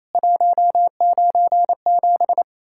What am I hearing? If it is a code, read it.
Morse code, heard as a keyed tone, about 28 wpm, 700 Hz. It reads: 197
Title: Worked All Europe DX Contest - QTC Exchange - 28wpm